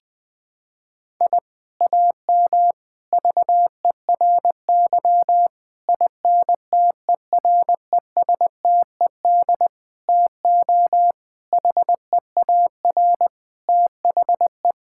Morse code, heard as a keyed tone, about 19 wpm, 700 Hz. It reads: I AM VERY INTERESTED TO HEAR THE
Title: Sentences from Top 1000 Words - Review Part 6 - 20wpm